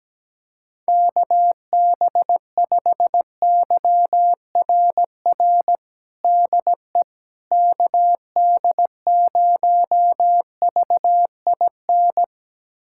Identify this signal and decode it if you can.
Morse code, heard as a keyed tone, about 17 wpm, 700 Hz. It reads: KB5YRR DE KD0VIN
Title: USA Callsign DE Callsign 17wpm